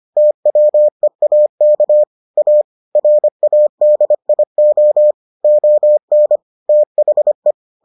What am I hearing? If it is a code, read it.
Morse code, heard as a keyed tone, about 25 wpm, 600 Hz. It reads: TWEAK A RADIO ON THE